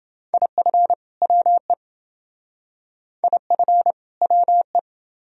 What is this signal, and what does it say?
Morse code, heard as a keyed tone, about 28 wpm, 700 Hz. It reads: IF WE IF WE